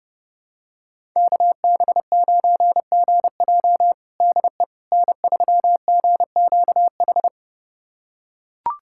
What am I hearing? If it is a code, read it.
Morse code, heard as a keyed tone, about 31 wpm, 700 Hz. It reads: KB9GJ DE N3GQH
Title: USA Callsign DE Callsign - 5 Characters - 30wpm